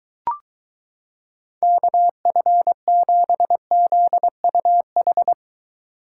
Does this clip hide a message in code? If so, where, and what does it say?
Yes: Morse code from 1.6 to 5.3 s, heard as a keyed tone, about 22 wpm, 700 Hz. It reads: KF7ZUH